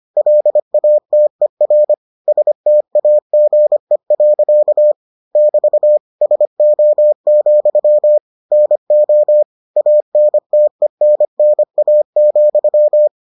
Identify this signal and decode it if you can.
Morse code, heard as a keyed tone, about 25 wpm, 600 Hz. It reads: LATER STAGE. <BT> SO, NO ANTENNA,